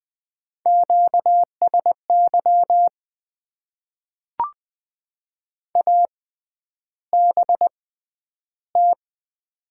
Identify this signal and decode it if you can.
Morse code, heard as a keyed tone, about 20 wpm, spaced as for 10 wpm, 700 Hz. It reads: QSY ABT
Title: Top 90 QSO Elements - 6x Character Spacing - 20wpm